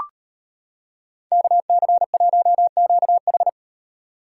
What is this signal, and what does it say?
Morse code, heard as a keyed tone, about 38 wpm, 700 Hz. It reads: KC1QH